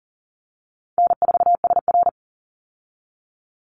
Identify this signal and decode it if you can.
Morse code, heard as a keyed tone, about 40 wpm, 700 Hz. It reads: N4SR
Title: Easy Call Signs Repeated 3x - 40wpm